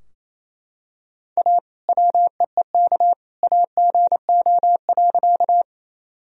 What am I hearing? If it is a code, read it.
Morse code, heard as a keyed tone, about 27 wpm, 700 Hz. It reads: A WEEK AGO.
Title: Sentences from Top 1000 Words - Review Part 3 - 28wpm